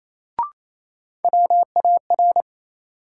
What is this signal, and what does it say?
Morse code, heard as a keyed tone, about 28 wpm, 700 Hz. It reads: WAR
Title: Sets of 5 Words - New Word Drill - 28wpm